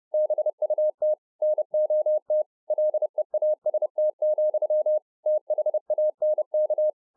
Morse code, heard as a keyed tone, about 30 wpm, 610 Hz. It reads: BUT NOT LEAST, THANK